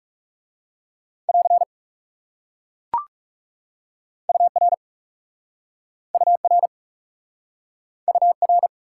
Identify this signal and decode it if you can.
Morse code, heard as a keyed tone, about 40 wpm, 700 Hz. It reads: <AR> UR UR UR